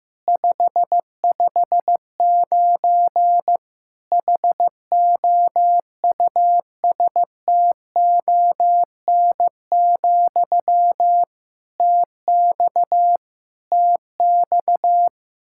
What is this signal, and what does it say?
Morse code, heard as a keyed tone, about 15 wpm, 700 Hz. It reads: 559 HOUSTON, TX TX
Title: K1USN SST QSO Practice 15wpm